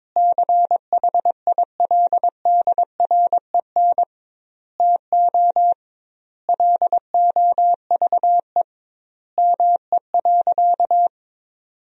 Morse code, heard as a keyed tone, about 21 wpm, 700 Hz. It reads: CHILDREN TO LOVE ME.
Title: 2x Word-Spacing - Sentences from Top 400 Words - 22wpm